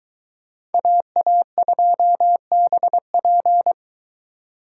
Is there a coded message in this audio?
Yes: Morse code from 0.7 to 3.7 s, heard as a keyed tone, about 23 wpm, 700 Hz. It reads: AA2BP